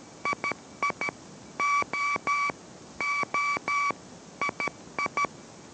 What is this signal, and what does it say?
Morse code, heard as a keyed tone, about 13 wpm, 2300 Hz, with noise. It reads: II O O II